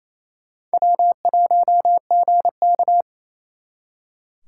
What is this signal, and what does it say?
Morse code, heard as a keyed tone, about 34 wpm, 700 Hz. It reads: W1GK
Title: Speed Racing - USA 1x2 Call Signs 28wpm